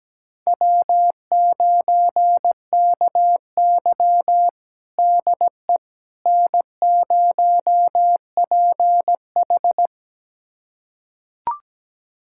Morse code, heard as a keyed tone, about 17 wpm, 700 Hz. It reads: W9KY DE N0PH
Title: USA Callsign DE Callsign - 4 Characters - 17wpm